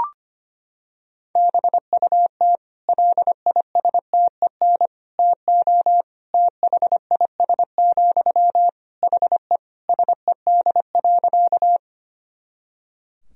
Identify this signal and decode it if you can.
Morse code, heard as a keyed tone, about 25 wpm, 700 Hz. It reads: BUT LISTEN TO THIS, HE SED.
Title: Sentences from Top 400 Words and Abbreviations 25wpm